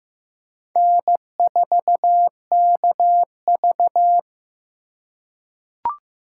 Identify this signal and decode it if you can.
Morse code, heard as a keyed tone, about 15 wpm, 700 Hz. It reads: N4KV
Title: USA Call Signs - 4 Characters - 3x Character Spacing - 15wpm